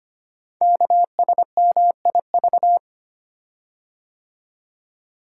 Morse code, heard as a keyed tone, about 25 wpm, 700 Hz. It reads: KSMIV
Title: Code Groups - 5 Letters - 25wpm